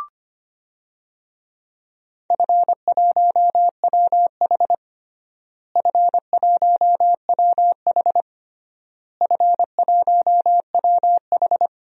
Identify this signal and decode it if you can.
Morse code, heard as a keyed tone, about 25 wpm, 700 Hz. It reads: F1WH F1WH F1WH
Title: DXCC-100 Call Signs - 4 Characters - Repeated 3x - 25wpm